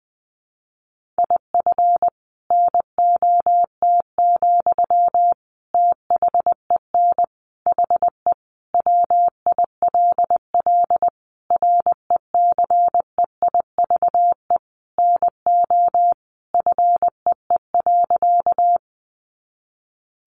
Morse code, heard as a keyed tone, about 20 wpm, 700 Hz. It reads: IF NOT, THEN HE WILL RECEIVE NO FEE.
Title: Sentences from Top 1500 Words 20wpm